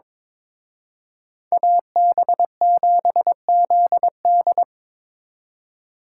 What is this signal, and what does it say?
Morse code, heard as a keyed tone, about 26 wpm, 700 Hz. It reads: AB7ZD
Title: Speed Racing - USA 2x2 Call Signs 22wpm